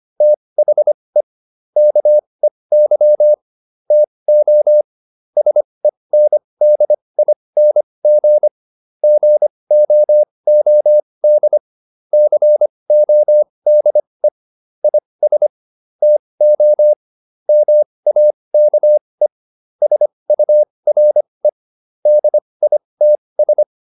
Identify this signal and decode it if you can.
Morse code, heard as a keyed tone, about 25 wpm, spaced as for 20 wpm, 600 Hz. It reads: THE KEY TO SENDING GOOD CODE IS TO MAKE SURE DITS